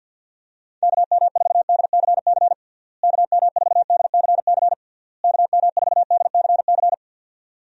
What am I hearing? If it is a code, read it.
Morse code, heard as a keyed tone, about 51 wpm, 700 Hz. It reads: KM4DCC KM4DCC KM4DCC
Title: USA Call Signs - 6 Characters - Repeated 3x - 50wpm